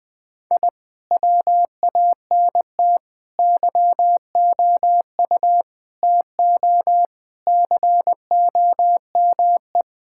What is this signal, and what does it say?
Morse code, heard as a keyed tone, about 19 wpm, 700 Hz. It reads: I WANT YOU TO COME
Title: Sentences with 2-Word Phrases - 20wpm